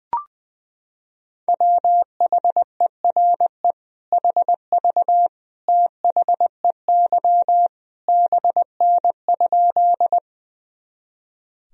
Morse code, heard as a keyed tone, about 20 wpm, 700 Hz. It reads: WHERE HV THEY BN?
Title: Sentences from Top 200 Words and Abbreviations 20wpm